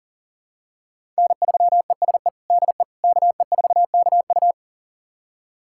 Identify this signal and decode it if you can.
Morse code, heard as a keyed tone, about 37 wpm, 700 Hz. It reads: N3ESE DE KE4KU
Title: USA Callsign DE Callsign - 5 Characters - 40wpm